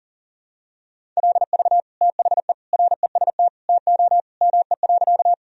Morse code, heard as a keyed tone, about 40 wpm, 700 Hz. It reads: LV THE REST TO ME.
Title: Sentences from Top 500 Words and Abbreviations 40wpm